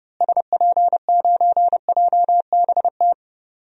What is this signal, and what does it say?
Morse code, heard as a keyed tone, about 30 wpm, 700 Hz. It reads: SP9JBT
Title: DXCC-100 Call Signs - 6 Characters - Repeated 3x - 30wpm